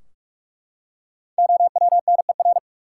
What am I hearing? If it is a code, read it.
Morse code, heard as a keyed tone, about 41 wpm, 700 Hz. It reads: OWNER
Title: Five Letter Words - 45wpm